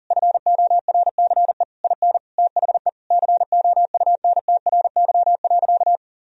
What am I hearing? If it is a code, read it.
Morse code, heard as a keyed tone, about 40 wpm, 700 Hz. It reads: FORCE IN THE COUNTRY.
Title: Sentences from Top 1000 Words - Review Part 4 - 40wpm